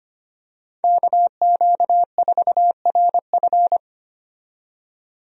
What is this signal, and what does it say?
Morse code, heard as a keyed tone, about 25 wpm, 700 Hz. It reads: KQ4RF
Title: USA Call Signs - 5 Characters - 25wpm